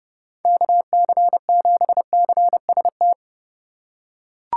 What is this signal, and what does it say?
Morse code, heard as a keyed tone, about 32 wpm, 700 Hz. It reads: KC7CST
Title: USA Call Signs - 6 Characters - 4x Character Spacing - 30wpm